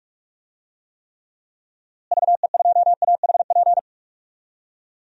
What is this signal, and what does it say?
Morse code, heard as a keyed tone, about 41 wpm, 700 Hz. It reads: VE2AHP
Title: DXCC-100 Call Signs - 6 Characters - Repeated 2x - 45wpm